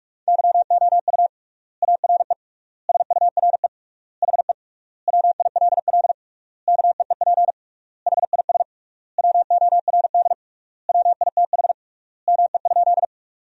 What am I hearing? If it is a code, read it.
Morse code, heard as a keyed tone, about 46 wpm, 700 Hz. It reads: YOU ARE SURE HE WILL KEEP HIS WORD WITH ME?
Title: 3x Word-Spacing - Sentences from Top 300 Words - 45wpm